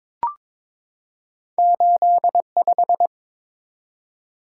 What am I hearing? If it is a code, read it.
Morse code, heard as a keyed tone, about 22 wpm, 700 Hz. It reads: 85
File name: practice / 22wpm / Two Letter-Number Combinations - 22wpm